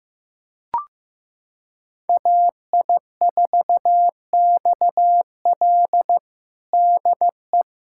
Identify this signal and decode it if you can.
Morse code, heard as a keyed tone, about 15 wpm, 700 Hz. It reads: AI4XL DE
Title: USA Callsign DE Callsign - 5 Characters - 15wpm